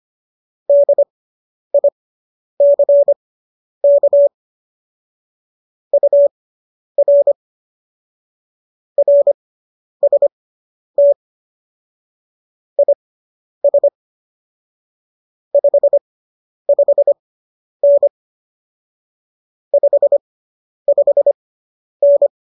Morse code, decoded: DICK UR RST IS 55N 55N

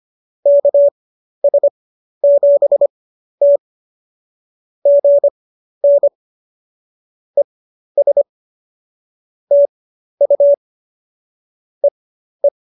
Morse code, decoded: KS7T GN ES TU EE

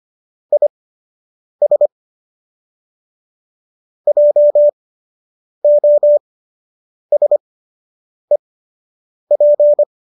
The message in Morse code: IS JOSEP

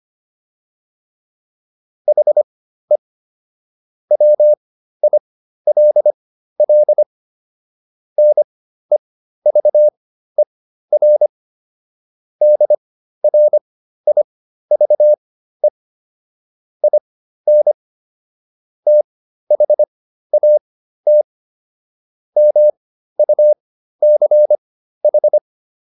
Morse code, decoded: HE WILL NEVER DRIVE IN THAT MUCH

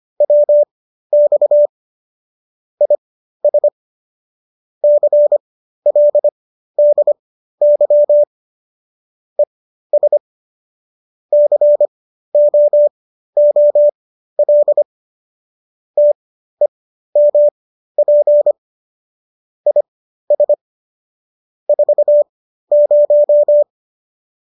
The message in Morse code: WX IS CLDY ES COOL TEMP IS 40

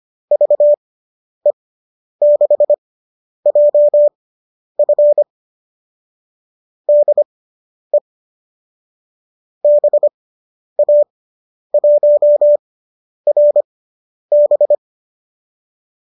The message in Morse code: VE6JF DE BA1RB

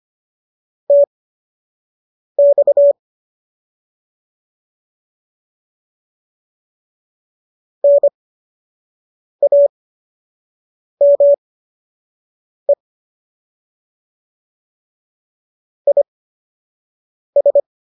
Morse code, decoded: TX NAME IS

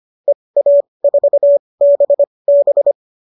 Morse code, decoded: EA4BB

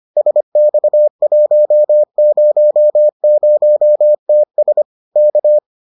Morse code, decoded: SX100TS K

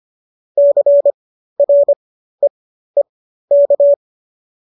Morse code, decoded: CREEK

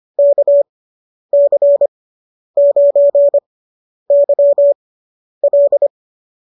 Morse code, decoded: KC9YL

